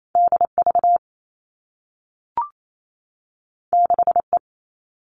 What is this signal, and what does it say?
Morse code, heard as a keyed tone, about 28 wpm, 700 Hz. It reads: DV 6E
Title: All Cut Numbers - 2-Digit 28wpm